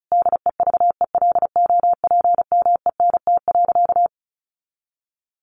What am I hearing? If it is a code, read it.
Morse code, heard as a keyed tone, about 34 wpm, 700 Hz. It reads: DEVELOPMENT.
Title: Sentences from Top 1400 Words 35wpm